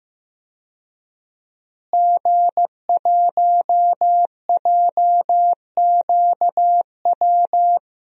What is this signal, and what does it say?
Morse code, heard as a keyed tone, about 15 wpm, 700 Hz. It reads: G1JQW
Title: DXCC-100 Call Signs - 5 Characters - Repeated 3x - 15wpm